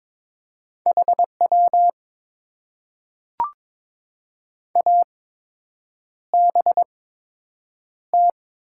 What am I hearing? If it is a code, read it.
Morse code, heard as a keyed tone, about 22 wpm, spaced as for 8 wpm, 700 Hz. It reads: HW ABT
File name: Top 90 QSO Elements - 8x Character Spacing - 22wpm